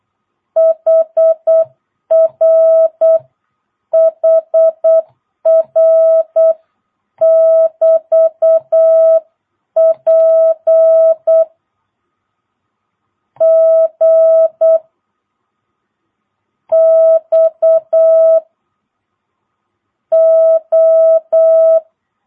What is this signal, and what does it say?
Morse code, heard as a keyed tone, about 8 wpm, 660 Hz. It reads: HR HR <BT> P G X O
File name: jh8cbh / 2022cw